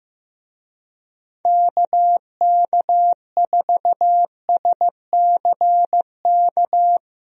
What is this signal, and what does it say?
Morse code, heard as a keyed tone, about 15 wpm, 700 Hz. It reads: KK4SCK
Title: USA Call Signs - 6 Characters - 15wpm